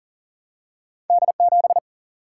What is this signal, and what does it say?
Morse code, heard as a keyed tone, about 40 wpm, 700 Hz. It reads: D7
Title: Single-Letter Single-Number 40wpm